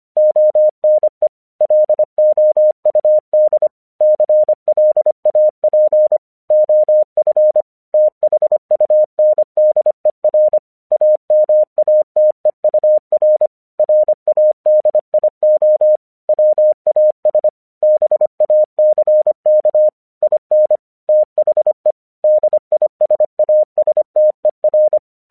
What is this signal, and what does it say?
Morse code, heard as a keyed tone, about 25 wpm, 610 Hz. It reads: ONE LOUD CLAP OF THUNDER AMATEUR RADIO WAS BACK IN THE DISASTER
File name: 1 English 25wpm / AQC07